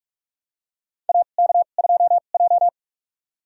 Morse code, heard as a keyed tone, about 44 wpm, spaced as for 32 wpm, 700 Hz. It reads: AK2J
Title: USA Call Signs - 2x Character Spacing - 45wpm